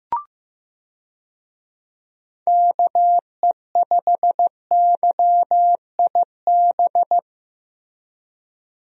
Morse code, heard as a keyed tone, about 15 wpm, 700 Hz. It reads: KE5YIB
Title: USA Call Signs - 6 Characters - Repeated 3x - 15wpm